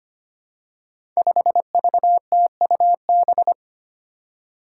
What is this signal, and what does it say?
Morse code, heard as a keyed tone, about 25 wpm, 700 Hz. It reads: 5VTUB
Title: Five Challenging Letter-Number Combinations 25wpm